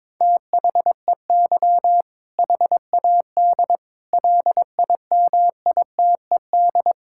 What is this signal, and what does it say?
Morse code, heard as a keyed tone, about 21 wpm, 700 Hz. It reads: THEY HAD LIMITED
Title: Sentences from Top 1000 Words - Review Part 6 - 22wpm